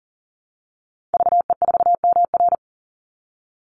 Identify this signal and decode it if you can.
Morse code, heard as a keyed tone, about 39 wpm, 700 Hz. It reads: VE4MR
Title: Call Signs Repeated 2x - 40wpm